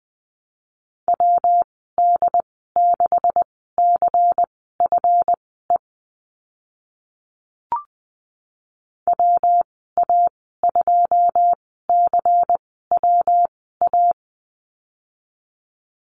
Morse code, heard as a keyed tone, about 20 wpm, spaced as for 14 wpm, 700 Hz. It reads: WD6CFE WA2CWA
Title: 140 Lesson 23g - Letter f - Call Signs 20wpm